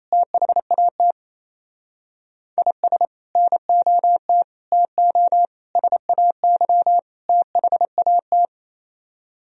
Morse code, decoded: THAT IS NOT TO SAY THAT

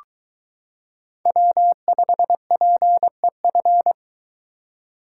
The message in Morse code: W5PEF

